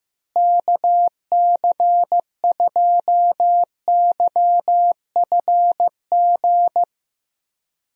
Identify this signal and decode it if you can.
Morse code, heard as a keyed tone, about 15 wpm, 700 Hz. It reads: KC2YFG
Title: USA Call Signs - 6 Characters - Repeated 2x - 15wpm